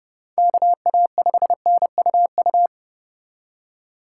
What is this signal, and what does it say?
Morse code, heard as a keyed tone, about 39 wpm, 700 Hz. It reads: KA5NUU